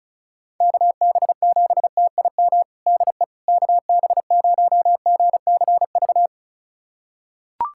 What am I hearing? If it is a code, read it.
Morse code, heard as a keyed tone, about 34 wpm, 700 Hz. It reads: KB7TIM DE KB0GCV
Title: USA Callsign DE Callsign - 6 Characters - 35wpm